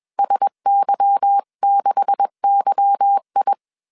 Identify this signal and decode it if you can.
Morse code, heard as a keyed tone, about 21 wpm, 800 Hz. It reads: SY6YI